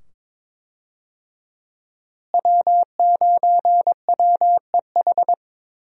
Morse code, heard as a keyed tone, about 22 wpm, 700 Hz. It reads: W9WEH
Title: USA Call Signs - 5 Characters - Repeated 2x - 22wpm